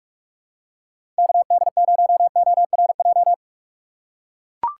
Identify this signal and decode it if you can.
Morse code, heard as a keyed tone, about 44 wpm, 700 Hz. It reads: KD0ORJ
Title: USA Call Signs - 6x Character Spacing - 45wpm